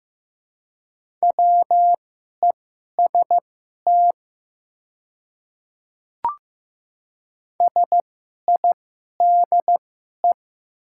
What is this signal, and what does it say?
Morse code, heard as a keyed tone, about 15 wpm, spaced as for 11 wpm, 700 Hz. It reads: WEST SIDE
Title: 2x Character Spacing - Top 500 Words - 15wpm